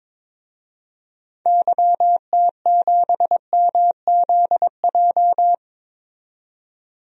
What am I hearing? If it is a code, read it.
Morse code, heard as a keyed tone, about 21 wpm, 700 Hz. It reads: YT7MZJ
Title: DXCC-100 Call Signs - 6 Characters - 22wpm